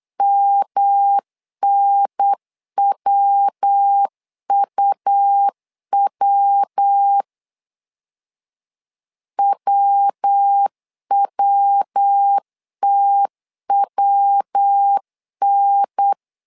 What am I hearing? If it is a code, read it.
Morse code, heard as a keyed tone, about 8 wpm, 800 Hz. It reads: MNWUW WWTWN
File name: LEZIONI / 3 MWRO / 3 MW 8wpm 2501